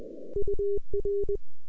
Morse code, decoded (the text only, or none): UR